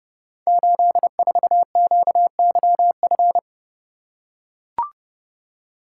Morse code, heard as a keyed tone, about 30 wpm, 700 Hz. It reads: 84QYF
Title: Code Groups - 5 Letter:Numbers - 30wpm